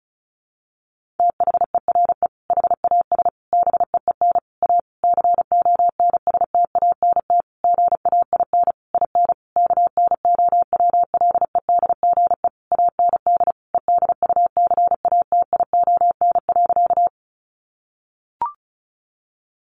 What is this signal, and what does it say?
Morse code, heard as a keyed tone, about 34 wpm, 700 Hz. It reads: THERE HAS BEEN A CONSTANT GAIN IN KNOWLEDGE AND EDUCATION.
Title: Sentences from Top 1500 Words 35wpm